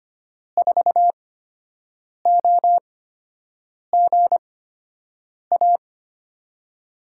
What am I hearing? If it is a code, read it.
Morse code, heard as a keyed tone, about 25 wpm, spaced as for 7 wpm, 700 Hz. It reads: 4OGA
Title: USA Call Signs - 5 Characters - 8x Character Spacing - 25wpm